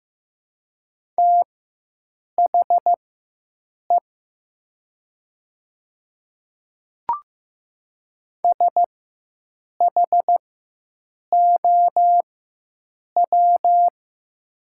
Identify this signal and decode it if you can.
Morse code, heard as a keyed tone, about 15 wpm, spaced as for 7 wpm, 700 Hz. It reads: THE SHOW
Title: 4x Character Spacing - Top 300 Words - 15wpm